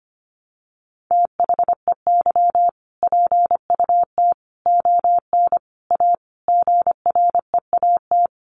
Morse code, heard as a keyed tone, about 25 wpm, 700 Hz. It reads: THEY PUT ON A GREAT
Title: Sentences from Top 1200 Words 25wpm